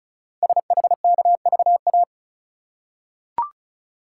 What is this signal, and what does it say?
Morse code, heard as a keyed tone, about 36 wpm, 700 Hz. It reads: SHKVA
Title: Code Groups - 5 Letters - 35wpm